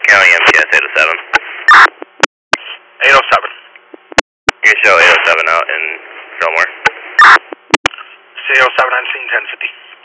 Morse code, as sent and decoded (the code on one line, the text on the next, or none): none